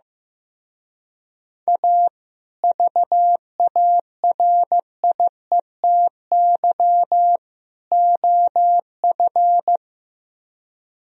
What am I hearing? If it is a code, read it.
Morse code, heard as a keyed tone, about 15 wpm, 700 Hz. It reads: A VARIETY OF
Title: Speed-Racing - Sets of 3 Words 15wpm